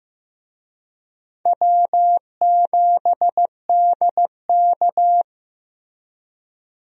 Morse code, heard as a keyed tone, about 15 wpm, 700 Hz. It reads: W7DK
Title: USA Call Signs - 4 Characters - Repeated 2x - 15wpm